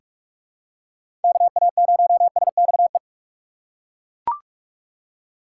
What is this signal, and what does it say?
Morse code, heard as a keyed tone, about 42 wpm, 700 Hz. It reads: KA0SXE
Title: USA Call Signs - 6 Characters - 4x Character Spacing - 45wpm